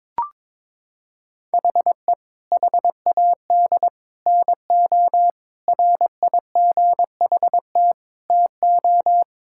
Morse code, heard as a keyed tone, about 22 wpm, 700 Hz. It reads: HE HAD NO RIGHT TO
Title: Sentences from Top 1000 Words - Review Part 5 - 22wpm